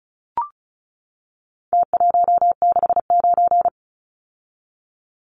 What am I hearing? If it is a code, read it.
Morse code, heard as a keyed tone, about 34 wpm, 700 Hz. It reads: T169
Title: Numbers - Contest Style - 4-Digit 35wpm